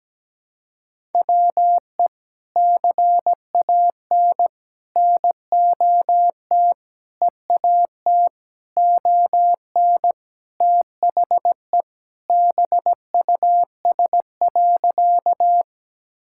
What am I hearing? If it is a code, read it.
Morse code, heard as a keyed tone, about 17 wpm, 700 Hz. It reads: WE CAN NOT EAT ON THE BUS.
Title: Sentences from Top 1000 Words - Review Part 6 - 17wpm